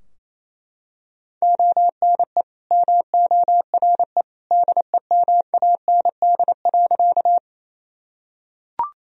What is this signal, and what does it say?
Morse code, heard as a keyed tone, about 27 wpm, 700 Hz. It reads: ONE MORE DEMAND.
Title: Sentences from Top 1000 Words - Review Part 4 - 28wpm